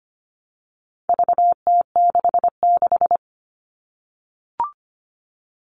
Morse code, heard as a keyed tone, about 25 wpm, 700 Hz. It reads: VT66
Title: All Cut Numbers - 4-Digit 25wpm